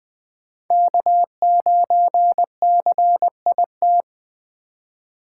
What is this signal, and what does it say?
Morse code, heard as a keyed tone, about 21 wpm, 700 Hz. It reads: K9CIT
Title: USA Call Signs - 5 Characters - 3x Character Spacing - 20wpm